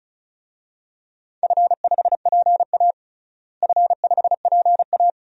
Morse code, heard as a keyed tone, about 36 wpm, 700 Hz. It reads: F5PA F5PA
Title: DXCC-100 Call Signs - 4 Characters - Repeated 2x - 35wpm